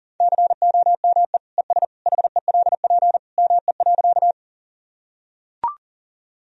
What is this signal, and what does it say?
Morse code, heard as a keyed tone, about 40 wpm, 700 Hz. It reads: COME ES HELP ME.
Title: Sentences from Top 200 Words and Abbreviations 40wpm